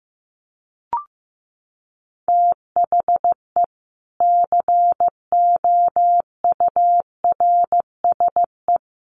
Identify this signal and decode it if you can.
Morse code, heard as a keyed tone, about 15 wpm, 700 Hz. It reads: THE COURSE
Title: Sets of 4 Words 15wpm